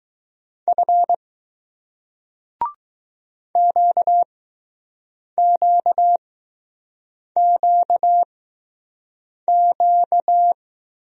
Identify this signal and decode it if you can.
Morse code, heard as a keyed tone, about 20 wpm, spaced as for 6 wpm, 700 Hz. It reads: F QQQQ